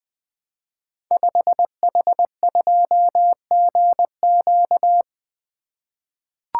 Morse code, heard as a keyed tone, about 20 wpm, 700 Hz. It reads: 5H2GQ
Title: Code Groups - 5 Letter:Numbers - 20wpm